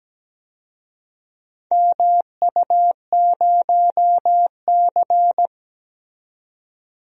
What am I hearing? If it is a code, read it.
Morse code, heard as a keyed tone, about 17 wpm, 700 Hz. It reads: MU0C
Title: DXCC-100 Call Signs - 4 Characters - 17wpm